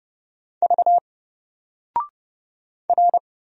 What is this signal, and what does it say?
Morse code, heard as a keyed tone, about 31 wpm, 700 Hz. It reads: V R
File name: Single-Letter-Number-30wpm